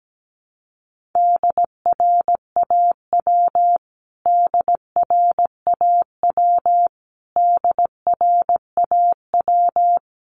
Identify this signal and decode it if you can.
Morse code, heard as a keyed tone, about 17 wpm, 700 Hz. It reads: DRAW DRAW DRAW
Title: Top 300 Words Repeated 17wpm